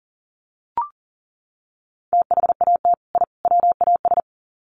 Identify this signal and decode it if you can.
Morse code, heard as a keyed tone, about 40 wpm, 700 Hz. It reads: THAT I WAS
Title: Sets of 3 Words 40wpm